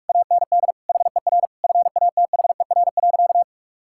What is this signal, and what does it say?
Morse code, heard as a keyed tone, about 46 wpm, 700 Hz. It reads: AND HER FATHER.
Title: Sentences from Top 1000 Words - Review Part 6 - 45wpm